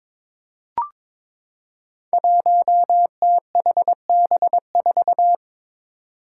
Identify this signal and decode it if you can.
Morse code, heard as a keyed tone, about 22 wpm, 700 Hz. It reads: 1THB4
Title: Five Challenging Letter-Number Combinations 22wpm